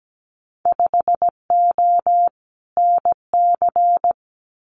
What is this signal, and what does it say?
Morse code, heard as a keyed tone, about 17 wpm, 700 Hz. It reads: 5O NC
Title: Winter Field Day Exchanges 17wpm